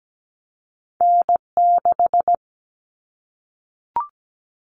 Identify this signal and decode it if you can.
Morse code, heard as a keyed tone, about 17 wpm, 700 Hz. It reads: N6
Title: Callsign Prefixes for Top 25 DXCC - Repeated 3x - 17wpm